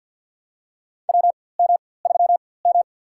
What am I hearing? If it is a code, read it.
Morse code, heard as a keyed tone, about 53 wpm, spaced as for 24 wpm, 700 Hz. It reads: WM3M
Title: USA Call Signs - 4x Character Spacing - 50wpm